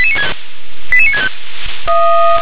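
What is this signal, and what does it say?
Five-tone selective call (ZVEI), heard as quick runs of five tones, tones of about 70 ms. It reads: 88965
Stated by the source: ZVEI five-tone selective call received with an SDR.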